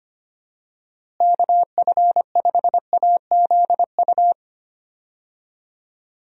Morse code, heard as a keyed tone, about 25 wpm, 700 Hz. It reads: KF5AZU
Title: USA Call Signs - 6 Characters - Repeated 2x - 25wpm